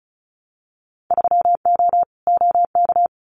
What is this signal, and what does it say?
Morse code, heard as a keyed tone, about 36 wpm, 700 Hz. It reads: 3O OK
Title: Winter Field Day Exchanges Encore 35wpm